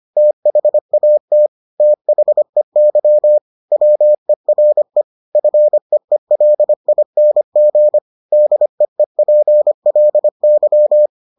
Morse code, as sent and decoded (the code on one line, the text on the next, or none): - .... .- - / - .... -.-- / .-- . .-. / ..-. . .-.. .. -. --. / -.. . .--. .-.. -.--
THAT THEY WERE FEELING DEEPLY